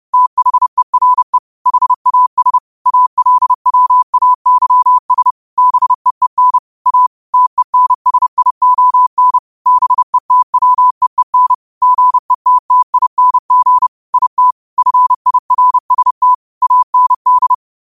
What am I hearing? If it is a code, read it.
Morse code, heard as a keyed tone, about 30 wpm, 1000 Hz. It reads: THERE HAS ALWAYS BEEN A TENSION BETWEEN GETTING IT FIRST AND
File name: qotdfiles / 30wpm